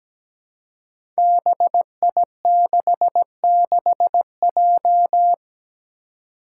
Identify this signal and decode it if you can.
Morse code, heard as a keyed tone, about 17 wpm, 700 Hz. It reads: BI66J
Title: Five Challenging Letter-Number Combinations 17wpm